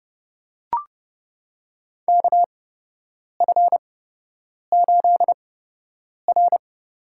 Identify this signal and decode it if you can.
Morse code, heard as a keyed tone, about 29 wpm, spaced as for 8 wpm, 700 Hz. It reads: KF8R
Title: USA Call Signs - 4 Characters - 8x Character Spacing - 30wpm